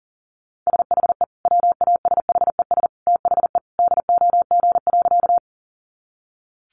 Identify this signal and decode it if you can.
Morse code, heard as a keyed tone, about 40 wpm, 700 Hz. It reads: SHE WASHES THE DOG.
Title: Easy Phrases 40wpm